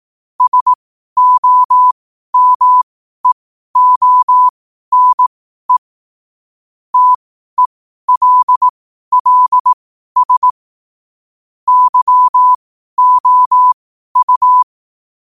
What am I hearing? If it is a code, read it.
Morse code, heard as a keyed tone, about 18 wpm, spaced as for 12 wpm, 1000 Hz. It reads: SOMEONE TELLS YOU
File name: qotdfiles / 12wpm